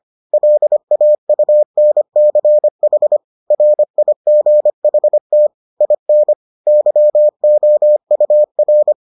Morse code, heard as a keyed tone, about 25 wpm, 600 Hz. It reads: LAUNCH RIGHT IN YOUR